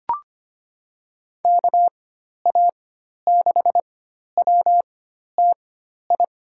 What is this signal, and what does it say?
Morse code, heard as a keyed tone, about 25 wpm, spaced as for 12 wpm, 700 Hz. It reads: KA6WTI
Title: USA Call Signs - 6 Characters - 4x Character Spacing - 25wpm